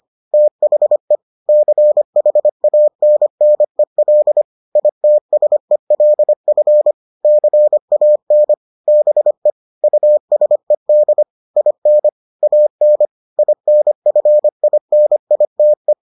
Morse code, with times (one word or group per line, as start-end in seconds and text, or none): 0.3-1.2 THE
1.5-4.4 CHANNEL
4.8-6.9 ITSELF
7.2-8.5 CAN
8.9-9.5 BE
9.8-11.2 USED
11.6-12.1 IN
12.4-13.1 AN
13.4-15.9 INFINITE